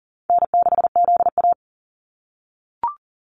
Morse code, heard as a keyed tone, about 40 wpm, 700 Hz. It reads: N6ZA
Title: NA SOTA Activator Callsigns 40wpm